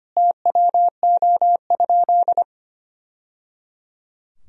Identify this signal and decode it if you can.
Morse code, heard as a keyed tone, about 25 wpm, 700 Hz. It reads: TWO?
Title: Sentences from Top 1000 Words - Review Part 2 - 25wpm